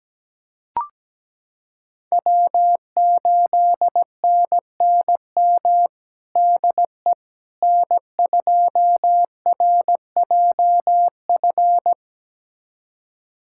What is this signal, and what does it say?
Morse code, heard as a keyed tone, about 17 wpm, 700 Hz. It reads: W8NNM DE N2RJF
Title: USA Callsign DE Callsign - 5 Characters - 17wpm